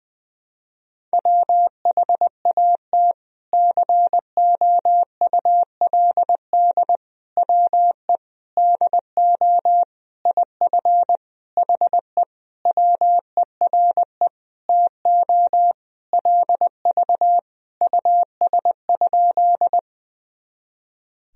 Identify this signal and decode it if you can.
Morse code, heard as a keyed tone, about 20 wpm, 700 Hz. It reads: WHAT COULD WE DO IF HE WERE TO LV US?
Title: Sentences from Top 400 Words and Abbreviations 20wpm